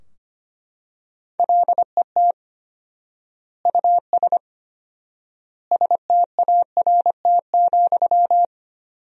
Morse code, heard as a keyed tone, about 25 wpm, 700 Hz. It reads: LET US START,
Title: 4x Word-Spacing - Sentences from Top 300 Words - 25wpm